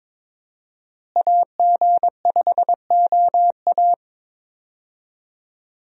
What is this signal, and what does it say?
Morse code, heard as a keyed tone, about 21 wpm, 700 Hz. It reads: AG5OA
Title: USA Call Signs - 5 Characters - Repeated 2x - 22wpm